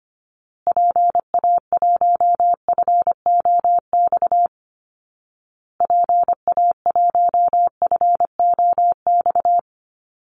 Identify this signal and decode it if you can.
Morse code, heard as a keyed tone, about 25 wpm, 700 Hz. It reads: PA1FOX PA1FOX
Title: Call Signs Repeated 2x - 25wpm